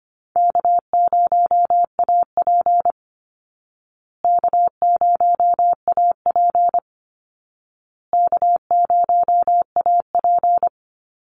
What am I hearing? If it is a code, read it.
Morse code, heard as a keyed tone, about 25 wpm, 700 Hz. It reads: K0AP K0AP K0AP
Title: Easy Call Signs Repeated 3x - 25wpm